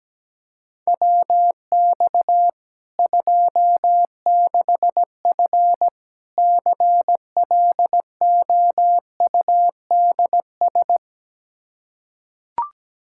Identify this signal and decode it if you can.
Morse code, heard as a keyed tone, about 17 wpm, 700 Hz. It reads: WX 26F CLOUDS
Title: Practice QSO - Exchange 2 - Info - 17wpm